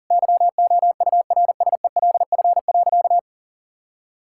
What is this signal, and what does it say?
Morse code, heard as a keyed tone, about 40 wpm, 700 Hz. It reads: YOURSELF.
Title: Sentences from Top 1600 Words 40wpm